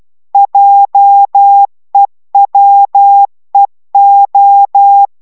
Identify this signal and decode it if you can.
Morse code, heard as a keyed tone, about 12 wpm, 800 Hz. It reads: JEWEO